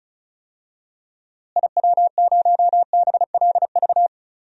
Morse code, heard as a keyed tone, about 34 wpm, 700 Hz. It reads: IW0BLV
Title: DXCC-100 Call Signs - 6 Characters - Repeated 2x - 35wpm